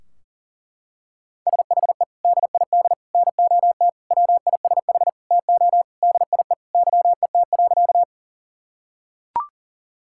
Morse code, heard as a keyed tone, about 40 wpm, 700 Hz. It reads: SHE DID NOT WISH TO DIE YET.
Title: Sentences from Top 1600 Words 40wpm